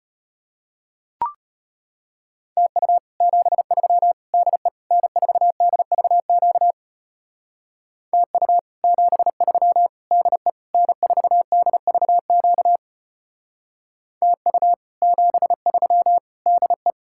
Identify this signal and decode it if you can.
Morse code, heard as a keyed tone, about 33 wpm, 700 Hz. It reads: TU 73 DE N4DVQ TU 73 DE N4DVQ TU 73 DE